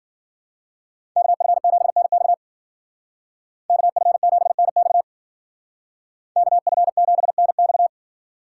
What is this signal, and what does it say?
Morse code, heard as a keyed tone, about 51 wpm, 700 Hz. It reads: KF7NX KF7NX KF7NX